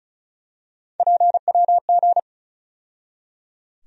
Morse code, heard as a keyed tone, about 36 wpm, 700 Hz. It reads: PWG
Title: USA Call Sign Postfixes - 3 Letters 35wpm